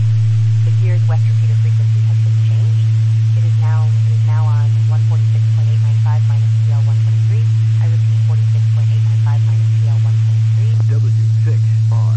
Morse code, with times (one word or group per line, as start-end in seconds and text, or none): none